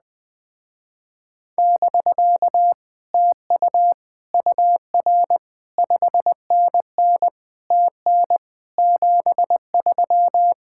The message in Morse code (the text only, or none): <BK> TU UR 5NN TN 73